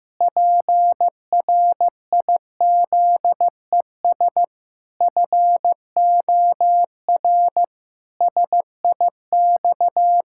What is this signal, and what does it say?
Morse code, heard as a keyed tone, about 15 wpm, 700 Hz. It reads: PRIZES FOR SIX
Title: Pangram Challenge v2 - 15wpm